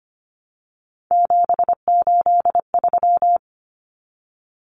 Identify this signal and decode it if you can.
Morse code, heard as a keyed tone, about 25 wpm, 700 Hz. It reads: 783
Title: Numbers - 3-Digit 25wpm